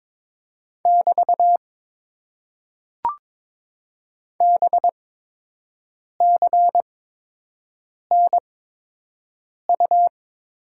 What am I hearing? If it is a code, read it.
Morse code, heard as a keyed tone, about 22 wpm, spaced as for 6 wpm, 700 Hz. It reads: <BT> BCNU
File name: Top 90 QSO Elements - 8x Character Spacing - 22wpm